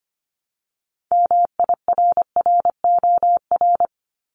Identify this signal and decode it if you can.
Morse code, heard as a keyed tone, about 25 wpm, 700 Hz. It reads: MIRROR
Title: Multi-letter Phonemes 25wpm